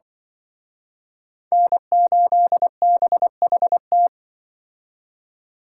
Morse code, heard as a keyed tone, about 25 wpm, 700 Hz. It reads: N8BHT